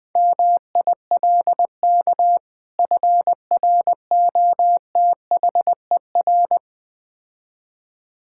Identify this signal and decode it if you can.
Morse code, heard as a keyed tone, about 20 wpm, 700 Hz. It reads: MILK FROTHER
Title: Difficult Words - Speed-Racing with Hints - 20wpm